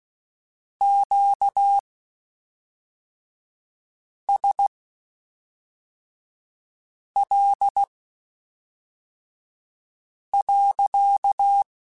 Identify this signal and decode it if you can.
Morse code, heard as a keyed tone, about 16 wpm, spaced as for 3 wpm, 800 Hz. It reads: QSL.
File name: Q Calls / Q Calls-mp3s / Q Calls sppc @5